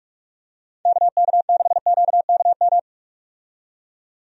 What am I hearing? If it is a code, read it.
Morse code, heard as a keyed tone, about 46 wpm, 700 Hz. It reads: KK6QKM
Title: USA Call Signs - 6 Characters - 45wpm